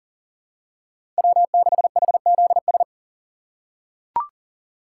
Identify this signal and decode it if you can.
Morse code, heard as a keyed tone, about 40 wpm, 700 Hz. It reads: W6HZS
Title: USA Call Signs - 5 Characters - 2x Character Spacing - 40wpm